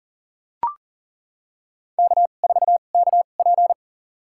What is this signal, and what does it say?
Morse code, heard as a keyed tone, about 40 wpm, spaced as for 29 wpm, 700 Hz. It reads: K4KP
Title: USA Call Signs - 4 Characters - 2x Character Spacing - 40wpm